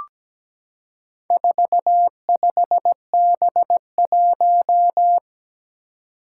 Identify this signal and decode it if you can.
Morse code, heard as a keyed tone, about 17 wpm, 700 Hz. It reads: 45B1
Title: Four Challenging Letter-Number Combinations 17wpm